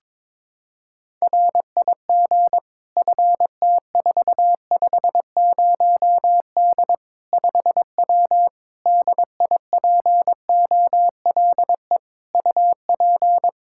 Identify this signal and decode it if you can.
Morse code, heard as a keyed tone, about 22 wpm, 700 Hz. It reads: RIG FT450D 5W DIPOLE UP
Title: Practice QSO - Exchange 2 - Info - 22wpm